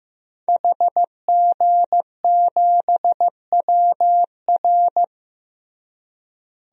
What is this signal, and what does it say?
Morse code, heard as a keyed tone, about 15 wpm, 700 Hz. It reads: HG7WR
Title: DXCC-100 Call Signs - 5 Characters - Repeated 2x - 15wpm